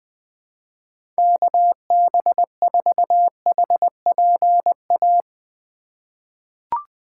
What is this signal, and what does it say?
Morse code, heard as a keyed tone, about 20 wpm, 700 Hz. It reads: KB4HPA
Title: USA Call Signs - 4x Character Spacing - 20wpm